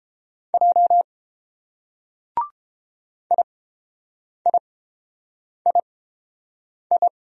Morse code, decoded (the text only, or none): J IIII